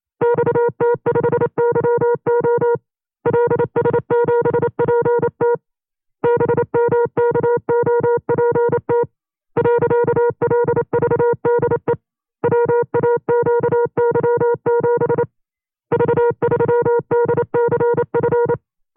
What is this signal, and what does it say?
Morse code, heard as a keyed tone, about 28 wpm, 480 Hz. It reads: XT5YO LS7PT BMKOPT .LVDE WAQY7 V3DCF
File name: 25WPM